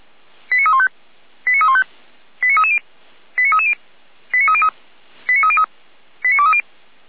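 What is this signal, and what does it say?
Five-tone selective call (ZVEI), heard as quick runs of five tones, tones of about 70 ms. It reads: 89315 89315 89330 89330 89303 89303 89230
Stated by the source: Five-tone ZVEI selective call received on an SDR.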